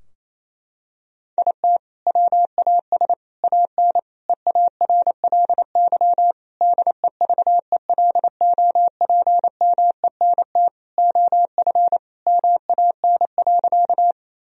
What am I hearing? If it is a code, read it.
Morse code, heard as a keyed tone, about 28 wpm, 700 Hz. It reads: IT WAS AN EARLY DEVELOPMENT OF MAN.
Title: Sentences from Top 1000 Words - Review Part 4 - 28wpm